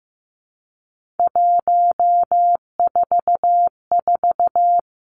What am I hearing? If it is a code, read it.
Morse code, heard as a keyed tone, about 15 wpm, 700 Hz. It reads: 144
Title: Three Challenging Letter-Number Combinations 15wpm